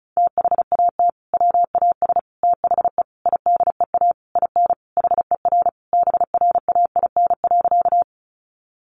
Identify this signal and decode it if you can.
Morse code, heard as a keyed tone, about 34 wpm, 700 Hz. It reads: THAT WAS THE IDEA IN HER BRAIN.
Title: Sentences from Top 1100 Words 35wpm